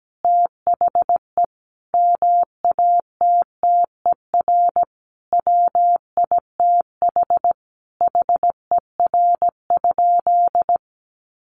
Morse code, heard as a keyed tone, about 17 wpm, 700 Hz. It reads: THE MATTER WITH HER?
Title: Sentences from Top 700 Words 17wpm